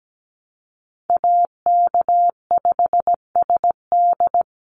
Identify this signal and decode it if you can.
Morse code, heard as a keyed tone, about 17 wpm, 700 Hz. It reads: AK5SD
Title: NA SOTA Chaser Callsigns 17wpm